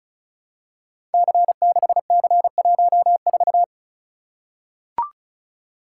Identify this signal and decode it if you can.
Morse code, heard as a keyed tone, about 35 wpm, 700 Hz. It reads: C6C14
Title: Code Groups - 5 Letter:Numbers - 35wpm